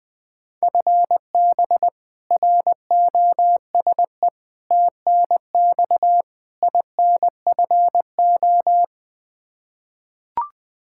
Morse code, heard as a keyed tone, about 20 wpm, 700 Hz. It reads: FB ROSE TNX INFO
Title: Practice QSO - Exchange 3 - More Info - 20wpm